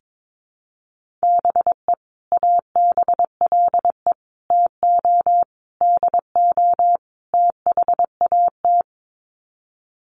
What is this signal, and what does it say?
Morse code, heard as a keyed tone, about 22 wpm, 700 Hz. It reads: BE ABLE TO DO THAT
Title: Sets of 5 Words Encore2 22wpm